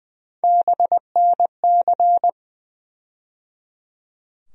Morse code, heard as a keyed tone, about 20 wpm, 700 Hz. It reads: BNC
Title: USA Call Sign Postfixes - 3 Letters 20wpm